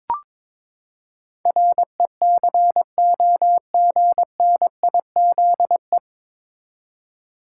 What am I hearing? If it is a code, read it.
Morse code, heard as a keyed tone, about 22 wpm, 700 Hz. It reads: RECOGNIZE
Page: Morse Code Ninja practice 22wpm